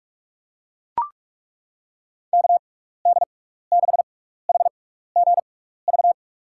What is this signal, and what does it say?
Morse code, heard as a keyed tone, about 41 wpm, spaced as for 15 wpm, 700 Hz. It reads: KD6HGV
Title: USA Call Signs - 6 Characters - 6x Character Spacing - 45wpm